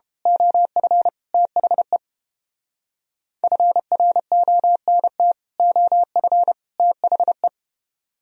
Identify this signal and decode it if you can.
Morse code, heard as a keyed tone, about 29 wpm, 700 Hz. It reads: OF THE FRONT OF THE